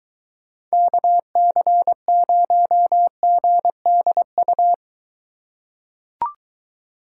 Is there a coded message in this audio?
Yes: Morse code from 0.7 to 4.7 s, heard as a keyed tone, about 21 wpm, 700 Hz. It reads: KC0GDU